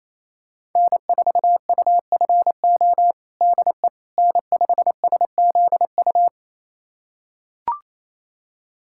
Morse code, heard as a keyed tone, about 28 wpm, 700 Hz. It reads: N4UFO DE N5SZU
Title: USA Callsign DE Callsign - 5 Characters - 28wpm